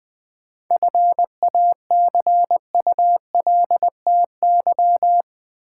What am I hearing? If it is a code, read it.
Morse code, heard as a keyed tone, about 20 wpm, 700 Hz. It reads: FACULTY
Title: 7 Letter Words - First 3 Letters Given - 20wpm